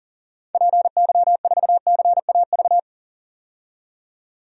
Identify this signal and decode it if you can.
Morse code, heard as a keyed tone, about 38 wpm, 700 Hz. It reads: PY4CAV
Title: DXCC-100 Call Signs - 6 Characters - 40wpm